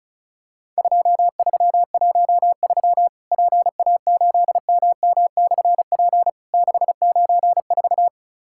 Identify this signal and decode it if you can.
Morse code, heard as a keyed tone, about 36 wpm, 700 Hz. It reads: 2313 PA8MM/P 694
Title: Worked All Europe DX Contest - QTC Exchange - 35wpm